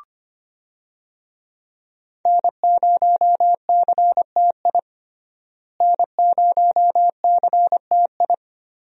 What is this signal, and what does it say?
Morse code, heard as a keyed tone, about 25 wpm, 700 Hz. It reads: N0CTI N0CTI
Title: USA Call Signs - 5 Characters - Repeated 3x - 25wpm